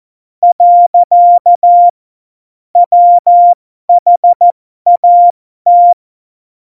Morse code, heard as a keyed tone, about 14 wpm, spaced as for 12 wpm, 700 Hz. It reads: . WHAT
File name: Morsekurs / Witze